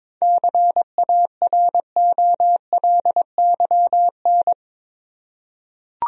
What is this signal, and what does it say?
Morse code, heard as a keyed tone, about 22 wpm, 700 Hz. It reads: CAROLYN
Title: USA Common Names - 22wpm